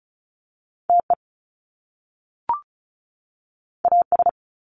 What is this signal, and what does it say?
Morse code, heard as a keyed tone, about 36 wpm, 700 Hz. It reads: TE AS
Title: Top Two Letter Combinations in Words 35wpm